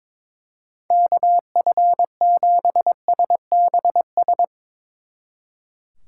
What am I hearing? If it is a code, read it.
Morse code, heard as a keyed tone, about 24 wpm, 700 Hz. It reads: KF7SBS
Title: Speed Racing - USA 2x3 Call Signs 22wpm